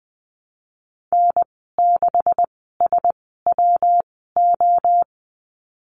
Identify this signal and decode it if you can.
Morse code, heard as a keyed tone, about 19 wpm, spaced as for 14 wpm, 700 Hz. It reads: N6SWO
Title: 161 Lesson 26g - Letter g - Call Signs 20wpm